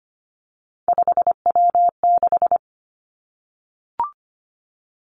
Morse code, heard as a keyed tone, about 25 wpm, 700 Hz. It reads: 5W6
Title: Three Challenging Letter-Number Combinations 25wpm